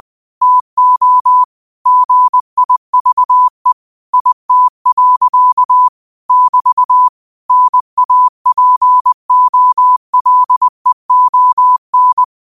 Morse code, heard as a keyed tone, about 20 wpm, 1000 Hz. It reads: TO GIVE IT. <BT> NAPOLEON